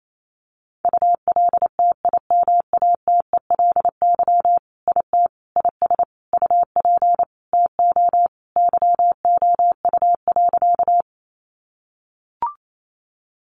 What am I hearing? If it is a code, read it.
Morse code, heard as a keyed tone, about 28 wpm, 700 Hz. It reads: ULTIMATELY IT IS UP TO YOU.
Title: Sentences from Top 1200 Words 28wpm